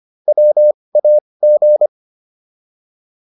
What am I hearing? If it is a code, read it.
Morse code, heard as a keyed tone, about 25 wpm, spaced as for 20 wpm, 600 Hz. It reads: WAG